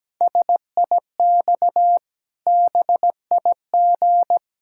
Morse code, heard as a keyed tone, about 17 wpm, 700 Hz. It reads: SIX BIG
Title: Pangram Challenge v2 - 17wpm